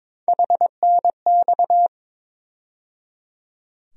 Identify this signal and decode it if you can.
Morse code, heard as a keyed tone, about 22 wpm, 700 Hz. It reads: HNX
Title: USA Call Sign Postfixes - 3 Letters 22wpm